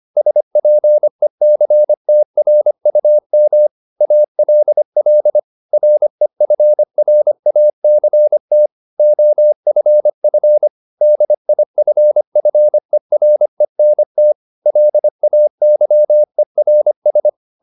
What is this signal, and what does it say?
Morse code, heard as a keyed tone, about 25 wpm, 600 Hz. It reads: SPECTRUM ALL REFRACT OFF DIFFERENT LAYERS
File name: SS211 25